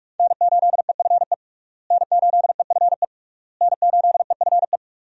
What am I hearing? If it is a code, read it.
Morse code, heard as a keyed tone, about 45 wpm, 700 Hz. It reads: N8EFE N8EFE N8EFE
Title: USA Call Signs - 5 Characters - Repeated 3x - 45wpm